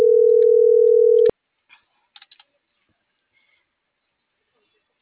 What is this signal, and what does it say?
Ringback tone (North America), 440 + 480 Hz.
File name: CENTRAL NMP / out melanie.diaznmp - .40596